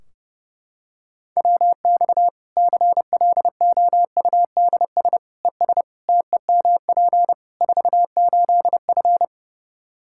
Morse code, heard as a keyed tone, about 30 wpm, 700 Hz. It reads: WX CLOUDS ES TEMP 48F
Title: Practice QSO - Exchange 2 - Info - 30wpm